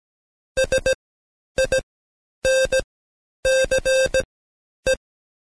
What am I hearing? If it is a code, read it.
Morse code, heard as a keyed tone, about 17 wpm, spaced as for 9 wpm, 530 Hz. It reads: SINCE
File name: Code Quick / AudioDownld2 / Seg131-Stepper4 9WPM